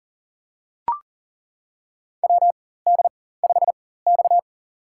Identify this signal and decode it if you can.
Morse code, heard as a keyed tone, about 37 wpm, spaced as for 18 wpm, 700 Hz. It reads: WD5X
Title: USA Call Signs - 4 Characters - 4x Character Spacing - 40wpm